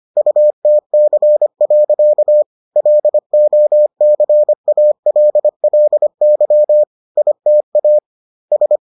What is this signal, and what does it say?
Morse code, heard as a keyed tone, about 25 wpm, 600 Hz. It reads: UTC. LOCALLY ITA S